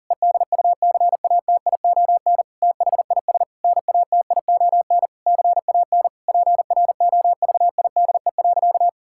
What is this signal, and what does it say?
Morse code, heard as a keyed tone, about 40 wpm, 700 Hz. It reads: EDUCATION THIS NATION CAN PROVIDE.
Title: Sentences from Top 1000 Words - Review Part 6 - 40wpm